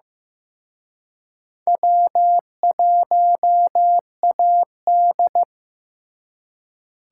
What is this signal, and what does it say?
Morse code, heard as a keyed tone, about 18 wpm, 700 Hz. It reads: W1AD
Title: Speed Racing - USA 1x2 Call Signs 15wpm